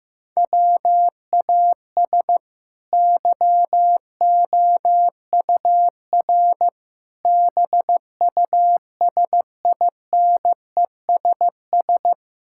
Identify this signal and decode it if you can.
Morse code, heard as a keyed tone, about 15 wpm, 700 Hz. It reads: WAS YOUR BUSINESS
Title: Sentences from Top 1000 Words - Review Part 5 - 15wpm